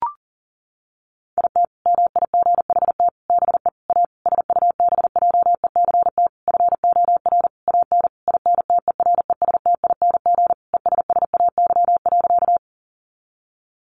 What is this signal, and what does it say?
Morse code, heard as a keyed tone, about 37 wpm, 700 Hz. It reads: IT MIGHT BE A SUBJECT FOR AN INTERESTING ESSAY.
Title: Sentences from Top 1500 Words 40wpm